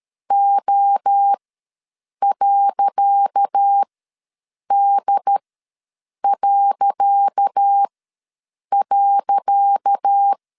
Morse code, heard as a keyed tone, about 13 wpm, spaced as for 7 wpm, 800 Hz. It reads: O.D..
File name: LEZIONI / 10 SIGNS / 10C DOT ws 12wpm 3548